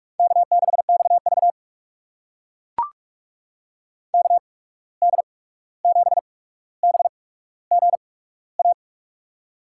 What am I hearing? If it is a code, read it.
Morse code, heard as a keyed tone, about 43 wpm, spaced as for 17 wpm, 700 Hz. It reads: K6XV KD7BGA
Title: USA Call Signs - 8x Character Spacing - 45wpm